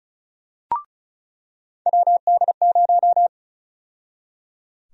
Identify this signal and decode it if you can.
Morse code, heard as a keyed tone, about 36 wpm, 700 Hz. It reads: WD0
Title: USA Call Sign Prefixes - 35wpm